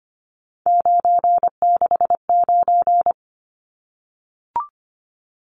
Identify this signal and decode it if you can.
Morse code, heard as a keyed tone, about 25 wpm, 700 Hz. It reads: 969
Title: Numbers - 3-Digit 25wpm